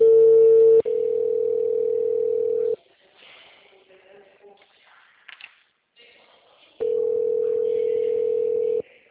Ringback tone (North America), 440 + 480 Hz.
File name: out- -2007- - - .1363